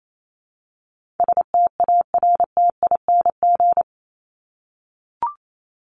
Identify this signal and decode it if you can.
Morse code, heard as a keyed tone, about 28 wpm, 700 Hz. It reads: STARTING
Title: ING Suffix - Top 100 Words with Base Words 28wpm